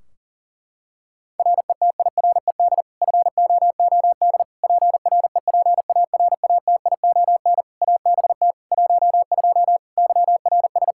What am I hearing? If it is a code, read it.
Morse code, heard as a keyed tone, about 40 wpm, 700 Hz. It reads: RETIRED FOOD PREPARATION ABT 12 YRS
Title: K1USN SST QSO Practice 40wpm